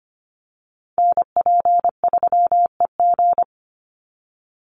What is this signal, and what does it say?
Morse code, heard as a keyed tone, about 25 wpm, 700 Hz. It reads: NP3EG
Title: USA Call Signs 25wpm